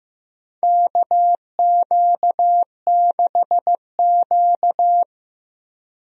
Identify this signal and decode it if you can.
Morse code, heard as a keyed tone, about 15 wpm, 700 Hz. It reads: KQ6Q
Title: USA Call Signs - 4 Characters - Repeated 3x - 15wpm